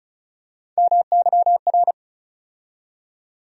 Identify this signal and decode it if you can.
Morse code, heard as a keyed tone, about 36 wpm, 700 Hz. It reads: MYR
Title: USA Call Sign Postfixes - 3 Letters 35wpm